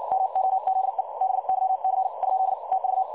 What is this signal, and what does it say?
Morse code, heard as a keyed tone, about 56 wpm, 760 Hz. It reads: MOZE BYC ? TV
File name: dlazaawansowanych / nagraniazpasma / websdr recording 3565.0kHz